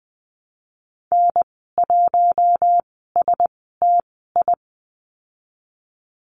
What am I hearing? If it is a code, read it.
Morse code, heard as a keyed tone, about 20 wpm, spaced as for 15 wpm, 700 Hz. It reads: N1STI
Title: 055 Lesson 10g - Letter r - Call Signs 20wpm